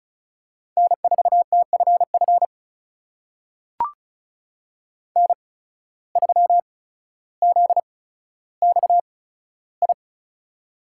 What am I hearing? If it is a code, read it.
Morse code, heard as a keyed tone, about 36 wpm, spaced as for 15 wpm, 700 Hz. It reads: N4TFF N3ZXI
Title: USA Call Signs - 5 Characters - 8x Character Spacing - 35wpm